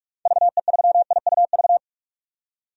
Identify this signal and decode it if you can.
Morse code, heard as a keyed tone, about 42 wpm, 700 Hz. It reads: VE3IUV
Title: DXCC-100 Call Signs - 6 Characters - 45wpm